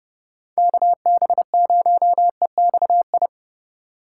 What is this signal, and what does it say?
Morse code, heard as a keyed tone, about 40 wpm, 700 Hz. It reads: KB0EXI